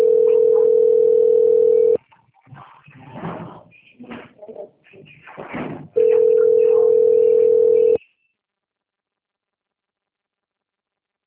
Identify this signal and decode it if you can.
Ringback tone (North America), 440 + 480 Hz.